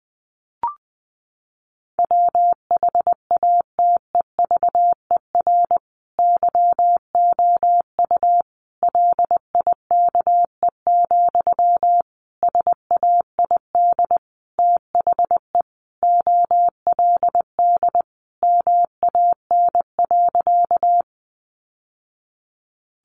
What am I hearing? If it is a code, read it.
Morse code, heard as a keyed tone, about 20 wpm, 700 Hz. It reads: WHATEVER YOU LIKE, SAID THE OLD MAN.
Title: Sentences from Top 1100 Words 20wpm